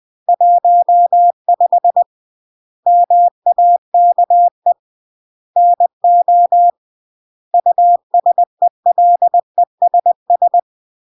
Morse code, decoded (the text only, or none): 15 MAKE NO USELESS